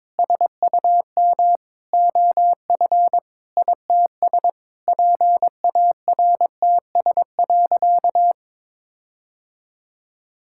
Morse code, heard as a keyed tone, about 22 wpm, 700 Hz. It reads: SUM OF ITS PARTS.
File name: practice / 22wpm / Sentences with 8-Word Phrases - 22wpm